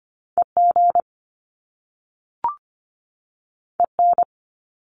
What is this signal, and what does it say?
Morse code, heard as a keyed tone, about 25 wpm, 700 Hz. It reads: EG EN
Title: Top Two Letter Combinations in Words 25wpm